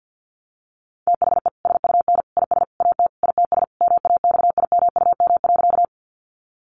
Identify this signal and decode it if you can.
Morse code, heard as a keyed tone, about 50 wpm, 700 Hz. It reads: THE SUN IS AT ITS MAXIMUM.
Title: Sentences from Top 1300 Words 50wpm